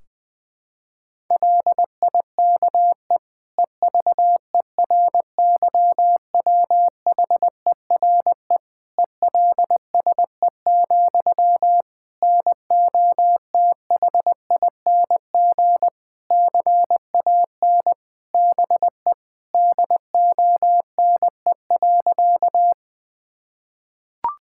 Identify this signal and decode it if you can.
Morse code, heard as a keyed tone, about 21 wpm, 700 Hz. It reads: LIKE EVERYWHERE ELSE, NOTHING CAN BE DONE.
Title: Sentences from Top 1600 Words 20wpm